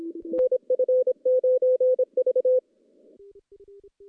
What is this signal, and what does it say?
Morse code, heard as a keyed tone, about 26 wpm, 510 Hz. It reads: NF9V
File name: contests / CW-SS-k5tr- -0400